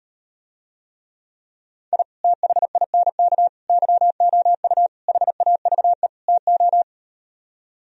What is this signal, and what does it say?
Morse code, heard as a keyed tone, about 38 wpm, 700 Hz. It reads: I THINK YOU HAVE TO